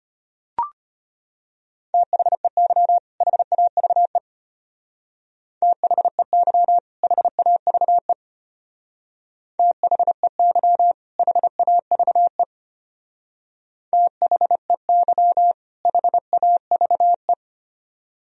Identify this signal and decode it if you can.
Morse code, heard as a keyed tone, about 31 wpm, 700 Hz. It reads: THEY HAVE THEY HAVE THEY HAVE THEY HAVE
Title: Speed-Racing - Sets of 2 Words 25wpm